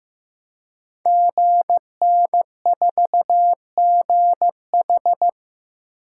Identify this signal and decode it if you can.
Morse code, heard as a keyed tone, about 15 wpm, 700 Hz. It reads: GN4GH
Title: Code Groups - 5 Letter:Numbers - 15wpm